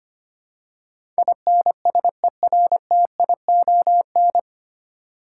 Morse code, heard as a keyed tone, about 25 wpm, 700 Hz. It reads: INSERTION
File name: practice / 25wpm / 9 Letter Words - First 3 Letters Given - 25wpm